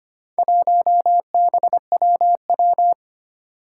Morse code, heard as a keyed tone, about 25 wpm, 700 Hz. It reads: 1BWW
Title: Four Challenging Letter-Number Combinations 25wpm